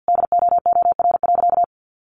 Morse code, heard as a keyed tone, about 49 wpm, 700 Hz. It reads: DOOR.
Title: Sentences from Top 300 Words 50wpm